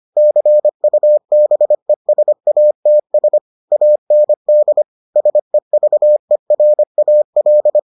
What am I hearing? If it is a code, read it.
Morse code, heard as a keyed tone, about 25 wpm, 600 Hz. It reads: CUBESATS AND SEVERAL